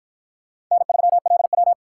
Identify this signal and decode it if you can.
Morse code, heard as a keyed tone, about 53 wpm, 700 Hz. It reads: N3LW